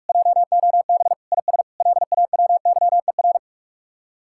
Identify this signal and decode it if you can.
Morse code, heard as a keyed tone, about 56 wpm, 700 Hz. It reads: JOB IS LAWYER
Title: Speed-Racing Common QSO Phrases - 45wpm